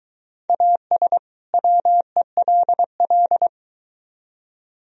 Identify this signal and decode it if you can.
Morse code, heard as a keyed tone, about 23 wpm, 700 Hz. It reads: AS WELL